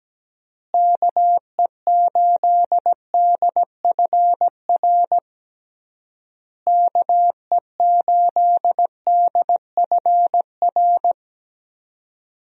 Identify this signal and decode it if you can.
Morse code, heard as a keyed tone, about 17 wpm, 700 Hz. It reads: KE8DFR KE8DFR
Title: USA Call Signs - 6 Characters - Repeated 3x - 17wpm